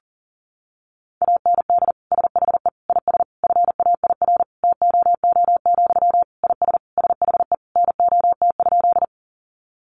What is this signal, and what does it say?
Morse code, heard as a keyed tone, about 40 wpm, 700 Hz. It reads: AND SHE IS FAIR TOO, IS SHE NOT?
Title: Sentences from Top 1300 Words 40wpm